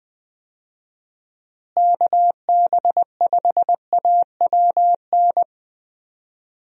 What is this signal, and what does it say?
Morse code, heard as a keyed tone, about 20 wpm, 700 Hz. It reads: KB5AWN
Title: USA Call Signs - 6 Characters - Repeated 3x - 20wpm